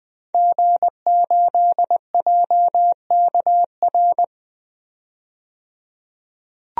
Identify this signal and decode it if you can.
Morse code, heard as a keyed tone, about 20 wpm, 700 Hz. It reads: G8JKR
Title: DXCC-100 Call Signs - 5 Characters - 20wpm